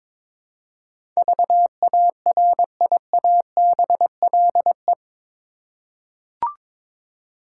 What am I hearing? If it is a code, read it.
Morse code, heard as a keyed tone, about 22 wpm, 700 Hz. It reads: VARIABLE
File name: practice / 22wpm / Eight Letter Words - 22wpm